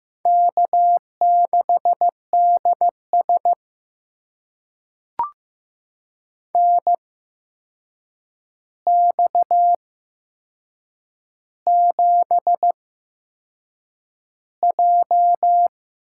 Morse code, heard as a keyed tone, about 15 wpm, spaced as for 6 wpm, 700 Hz. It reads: K6DS NX7J
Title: USA Call Signs - 4 Characters - 8x Character Spacing - 15wpm